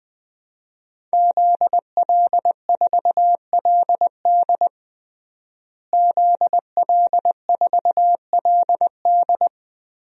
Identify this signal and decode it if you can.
Morse code, heard as a keyed tone, about 20 wpm, 700 Hz. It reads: ZL4LD ZL4LD
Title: DXCC-100 Call Signs - 5 Characters - Repeated 3x - 20wpm